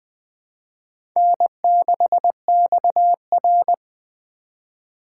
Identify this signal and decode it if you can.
Morse code, heard as a keyed tone, about 20 wpm, 700 Hz. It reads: N6XR
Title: USA Call Signs - 4 Characters - 4x Character Spacing - 20wpm